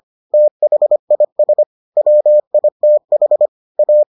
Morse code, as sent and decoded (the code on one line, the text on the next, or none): - .... .. ... / .-- .. - .... / .-
THIS WITH A